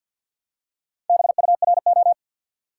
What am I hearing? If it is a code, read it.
Morse code, heard as a keyed tone, about 48 wpm, 700 Hz. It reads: BURO
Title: Top 180 QSO Elements 50wpm